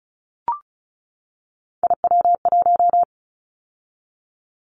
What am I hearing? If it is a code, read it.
Morse code, heard as a keyed tone, about 36 wpm, 700 Hz. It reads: IW1
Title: Callsign Prefixes for Top 25 DXCC - 35wpm